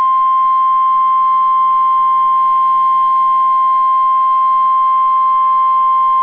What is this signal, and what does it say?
Emergency Alert System (SAME) tones (the 1050 Hz attention tone). It is heard from 0.3 to 5.4 s.